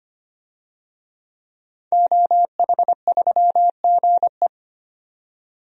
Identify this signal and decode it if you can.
Morse code, heard as a keyed tone, about 25 wpm, 700 Hz. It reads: OH3GE
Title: DXCC-100 Call Signs - 5 Characters - 25wpm